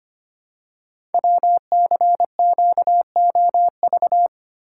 Morse code, heard as a keyed tone, about 25 wpm, 700 Hz. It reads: WCQOV
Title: Code Groups - 5 Letters - 25wpm